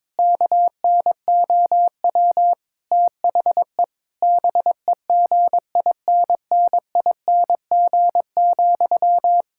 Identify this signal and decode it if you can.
Morse code, heard as a keyed tone, about 22 wpm, 700 Hz. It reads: KNOW THE BEGINNING,
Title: Sentences from Top 1000 Words - Review Part 6 - 22wpm